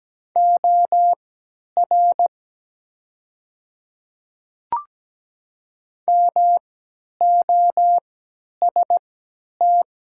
Morse code, heard as a keyed tone, about 17 wpm, spaced as for 10 wpm, 700 Hz. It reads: OR MOST
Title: 3x Character Spacing - Top 100 Words - 17wpm